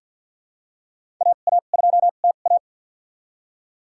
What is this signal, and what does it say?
Morse code, heard as a keyed tone, about 53 wpm, spaced as for 37 wpm, 700 Hz. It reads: AA2TA
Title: USA Call Signs - 5 Characters - 2x Character Spacing - 50wpm